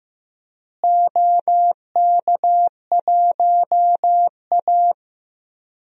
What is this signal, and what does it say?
Morse code, heard as a keyed tone, about 15 wpm, 700 Hz. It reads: OK1A
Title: DXCC-100 Call Signs - 4 Characters - Repeated 2x - 15wpm